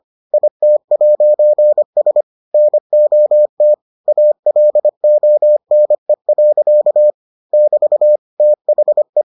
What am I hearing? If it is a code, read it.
Morse code, heard as a keyed tone, about 25 wpm, 600 Hz. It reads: IT'S NOT ALONE. <BT> THE